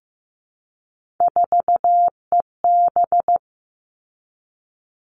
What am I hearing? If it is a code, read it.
Morse code, heard as a keyed tone, about 15 wpm, 700 Hz. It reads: 4EB
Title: All Cut Numbers - 3-Digit 15wpm